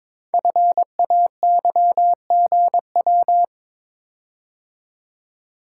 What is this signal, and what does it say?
Morse code, heard as a keyed tone, about 22 wpm, 700 Hz. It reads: FAYGW
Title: Code Groups - 5 Letters - 22wpm